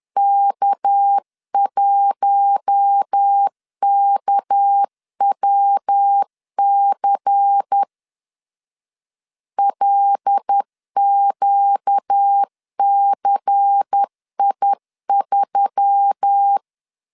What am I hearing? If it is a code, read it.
Morse code, heard as a keyed tone, about 10 wpm, 800 Hz. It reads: K1KWC LQCI3
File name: LEZIONI / 6 PKX0 / 6 PK 10wpm 2436